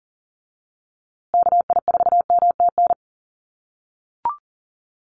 Morse code, heard as a keyed tone, about 40 wpm, 700 Hz. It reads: KI4MTN
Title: USA Call Signs Repeated 2x 40wpm